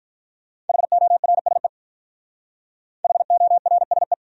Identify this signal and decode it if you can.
Morse code, heard as a keyed tone, about 48 wpm, 700 Hz. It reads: HORSE HORSE